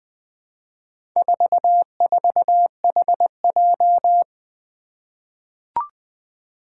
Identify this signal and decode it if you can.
Morse code, heard as a keyed tone, about 21 wpm, 700 Hz. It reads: 44HJ
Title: Four Challenging Letter-Number Combinations 20wpm